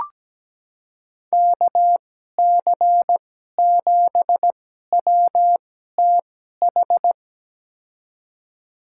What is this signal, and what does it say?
Morse code, heard as a keyed tone, about 17 wpm, spaced as for 12 wpm, 700 Hz. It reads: KC7WTH
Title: USA Call Signs - 6 Characters - 2x Character Spacing - 17wpm